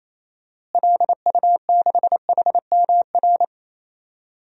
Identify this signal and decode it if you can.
Morse code, heard as a keyed tone, about 28 wpm, 700 Hz. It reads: LU6HMR
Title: DXCC-100 Call Signs - 6 Characters - Repeated 3x - 28wpm